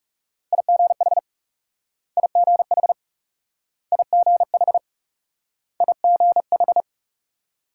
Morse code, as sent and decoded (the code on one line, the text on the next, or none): .. --. .... / .. --. .... / .. --. .... / .. --. ....
IGH IGH IGH IGH